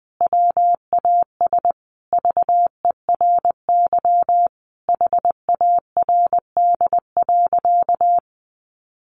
Morse code, heard as a keyed tone, about 20 wpm, 700 Hz. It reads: WAS VERY HARD.
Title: Sentences from Top 300 Words 20wpm